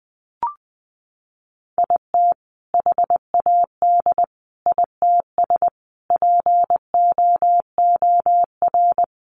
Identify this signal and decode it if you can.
Morse code, heard as a keyed tone, about 20 wpm, 700 Hz. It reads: IT HAD ITS POOR
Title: Sentences from Top 1000 Words 20wpm